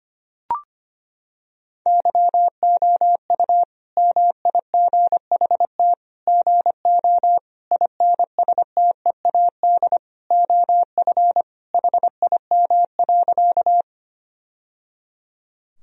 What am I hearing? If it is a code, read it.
Morse code, heard as a keyed tone, about 25 wpm, 700 Hz. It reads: YOU MIGHT GO INSTEAD OF HIM.
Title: Sentences from Top 1000 Words - Review Part 4 - 25wpm